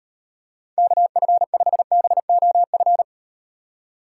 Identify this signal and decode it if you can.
Morse code, heard as a keyed tone, about 38 wpm, 700 Hz. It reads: KF5BOF